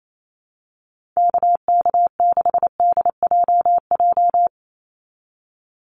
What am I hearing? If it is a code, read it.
Morse code, heard as a keyed tone, about 28 wpm, 700 Hz. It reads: KK6DJJ
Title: USA Call Signs 28wpm